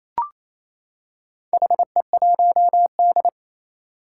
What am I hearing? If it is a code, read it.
Morse code, heard as a keyed tone, about 28 wpm, 700 Hz. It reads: HE1D
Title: Four Challenging Letter-Number Combinations 28wpm